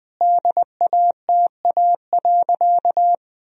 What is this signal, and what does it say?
Morse code, heard as a keyed tone, about 20 wpm, 700 Hz. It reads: DATA.
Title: Sentences from Top 1000 Words - Review Part 3 - 20wpm